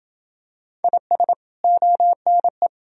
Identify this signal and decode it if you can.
Morse code, heard as a keyed tone, about 32 wpm, 700 Hz. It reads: IS ONE